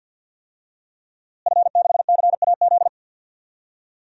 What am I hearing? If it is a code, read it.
Morse code, heard as a keyed tone, about 52 wpm, 700 Hz. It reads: F6CAZ
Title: DXCC-100 Call Signs - 5 Characters - Repeated 3x - 50wpm